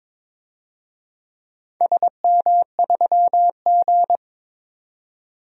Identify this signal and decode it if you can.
Morse code, heard as a keyed tone, about 22 wpm, 700 Hz. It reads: SM3G
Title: DXCC-100 Call Signs - 4 Characters - 22wpm